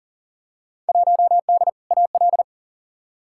Morse code, heard as a keyed tone, about 39 wpm, 700 Hz. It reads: 1D AL
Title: Field Day Exchanges - 40wpm